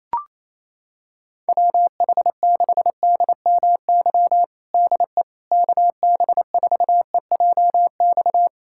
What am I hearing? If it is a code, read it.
Morse code, heard as a keyed tone, about 28 wpm, 700 Hz. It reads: WH6DMY DE KB4EJX
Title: USA Callsign DE Callsign - 6 Characters - 28wpm